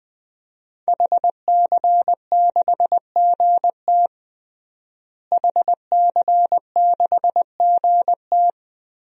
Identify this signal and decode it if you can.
Morse code, heard as a keyed tone, about 20 wpm, 700 Hz. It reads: HC6GT HC6GT
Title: DXCC-100 Call Signs - 5 Characters - Repeated 3x - 20wpm